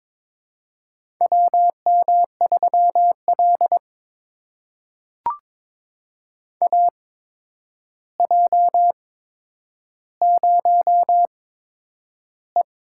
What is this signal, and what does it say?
Morse code, heard as a keyed tone, about 23 wpm, spaced as for 10 wpm, 700 Hz. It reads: WM3L AJ0E
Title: USA Call Signs - 4 Characters - 8x Character Spacing - 22wpm